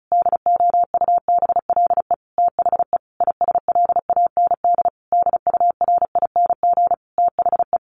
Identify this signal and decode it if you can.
Morse code, heard as a keyed tone, about 34 wpm, 700 Hz. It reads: DOUBLE THE ISLAND DURING THE
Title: Sentences from Top 1400 Words 35wpm